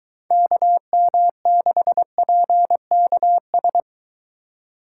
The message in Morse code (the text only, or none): KM6PKS